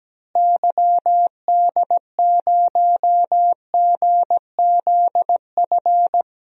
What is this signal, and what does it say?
Morse code, heard as a keyed tone, about 17 wpm, 700 Hz. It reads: YD0GZF
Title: DXCC-100 Call Signs - 6 Characters - Repeated 3x - 17wpm